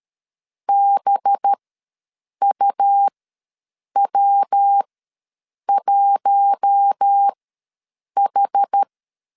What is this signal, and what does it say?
Morse code, heard as a keyed tone, about 13 wpm, spaced as for 7 wpm, 800 Hz. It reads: BUW1H
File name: LEZIONI / 5 LBVC / 5 LB ws 12wpm 3034